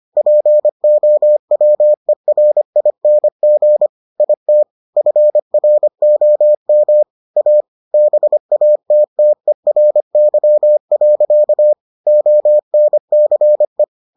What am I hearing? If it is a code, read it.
Morse code, heard as a keyed tone, about 25 wpm, 600 Hz. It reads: POWERING IT FROM A BATTERY. ONCE